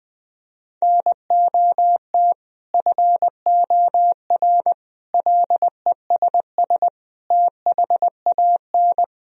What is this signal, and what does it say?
Morse code, heard as a keyed tone, about 20 wpm, 700 Hz. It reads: NOT FOR LESS THAN
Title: Sentences from Top 1000 Words - Review Part 3 - 20wpm